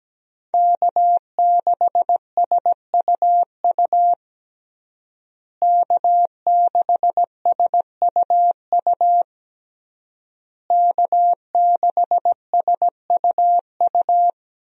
Morse code, heard as a keyed tone, about 17 wpm, 700 Hz. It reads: K6SUU K6SUU K6SUU
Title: USA Call Signs - 5 Characters - Repeated 3x - 17wpm